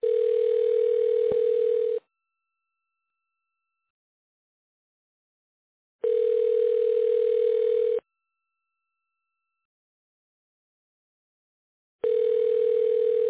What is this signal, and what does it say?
Ringback tone (North America), 440 + 480 Hz.